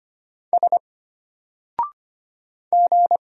Morse code, heard as a keyed tone, about 25 wpm, 700 Hz. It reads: S G
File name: Single-Letter-Number-25wpm